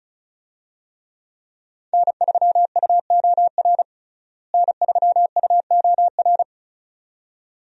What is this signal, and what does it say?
Morse code, heard as a keyed tone, about 36 wpm, 700 Hz. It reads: N3UOR N3UOR
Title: USA Call Signs - 5 Characters - Repeated 2x - 35wpm